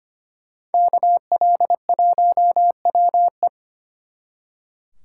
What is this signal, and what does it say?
Morse code, heard as a keyed tone, about 32 wpm, 700 Hz. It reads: KL1WE